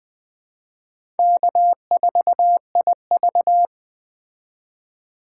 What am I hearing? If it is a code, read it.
Morse code, heard as a keyed tone, about 20 wpm, 700 Hz. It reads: K4IV
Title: Speed Racing - USA 1x2 Call Signs 17wpm